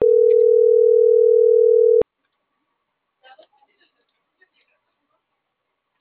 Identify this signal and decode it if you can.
Ringback tone (North America), 440 + 480 Hz.